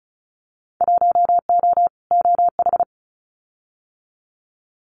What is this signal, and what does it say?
Morse code, heard as a keyed tone, about 36 wpm, 700 Hz. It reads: 1O OH
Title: Winter Field Day Exchanges 35wpm